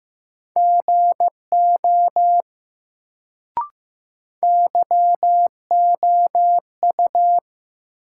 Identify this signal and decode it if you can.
Morse code, heard as a keyed tone, about 15 wpm, 700 Hz. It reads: GO YOU
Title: Sentences from Top 100 Words - New Word Drill - 15wpm